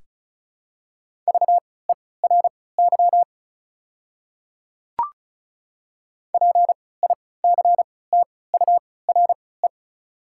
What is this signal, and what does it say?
Morse code, heard as a keyed tone, about 34 wpm, spaced as for 20 wpm, 700 Hz. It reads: VERY PICTURE
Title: 3x Character Spacing - Top 300 Words - 35wpm